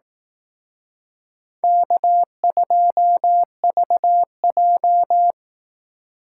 Morse code, heard as a keyed tone, about 18 wpm, 700 Hz. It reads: K2VJ
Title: Speed Racing - USA 1x2 Call Signs 15wpm